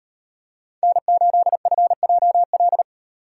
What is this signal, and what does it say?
Morse code, heard as a keyed tone, about 38 wpm, 700 Hz. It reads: N8FJL